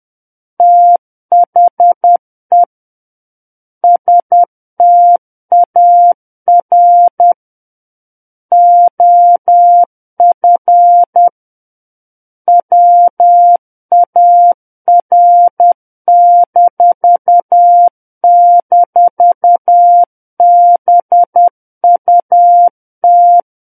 Morse code, heard as a keyed tone, about 10 wpm, 700 Hz. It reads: THE STAR OF WAR--BUT